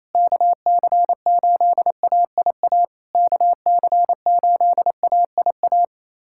Morse code, heard as a keyed tone, about 28 wpm, 690 Hz. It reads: KC8AIA KC8AIA
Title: SOTA QSO Practice 28wpm